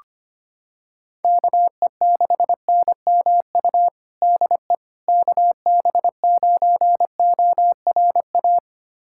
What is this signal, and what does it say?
Morse code, heard as a keyed tone, about 25 wpm, 700 Hz. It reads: KE6NMU DE KB9ORA
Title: USA Callsign DE Callsign 25wpm